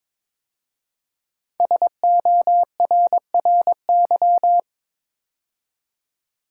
Morse code, heard as a keyed tone, about 22 wpm, 700 Hz. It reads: SORRY
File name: 5 Letter Words - First 3 Letters Given - 22wpm